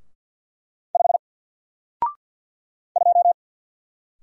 Morse code, heard as a keyed tone, about 50 wpm, 700 Hz. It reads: 5 2
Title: Single Letter-Number 50wpm